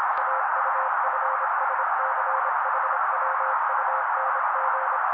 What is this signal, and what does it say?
Morse code, heard as a keyed tone, about 26 wpm, 530 Hz. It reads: AUFSCHWUNG
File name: daily / 25wpm noise